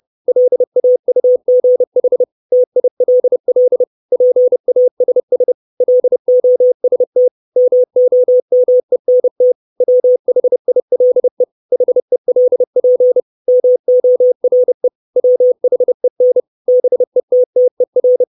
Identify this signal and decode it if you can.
Morse code, heard as a keyed tone, about 30 wpm, 500 Hz. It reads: LAUGH TILL PASS LOST MOMENT WHILE HELP MORE WHEN BETTER
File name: 632 Words 30WPM